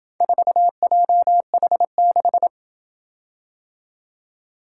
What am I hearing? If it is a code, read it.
Morse code, heard as a keyed tone, about 27 wpm, 700 Hz. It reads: 4JH6